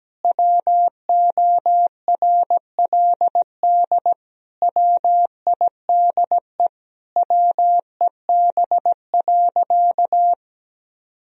Morse code, heard as a keyed tone, about 17 wpm, 700 Hz. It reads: WORLD WIDE WEB.
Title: Sentences from Top 1000 Words - Review Part 4 - 17wpm